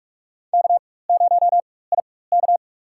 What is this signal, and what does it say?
Morse code, heard as a keyed tone, about 46 wpm, spaced as for 21 wpm, 700 Hz. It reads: K0IK
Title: USA Call Signs - 4 Characters - 4x Character Spacing - 45wpm